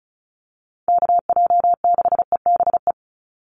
Morse code, heard as a keyed tone, about 34 wpm, 700 Hz. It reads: KJ6EBE
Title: USA Call Signs 35wpm